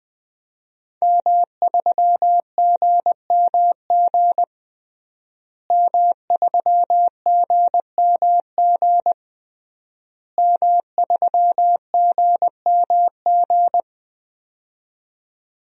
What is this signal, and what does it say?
Morse code, heard as a keyed tone, about 21 wpm, 700 Hz. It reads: M3GMG M3GMG M3GMG
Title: DXCC-100 Call Signs - 5 Characters - Repeated 3x - 20wpm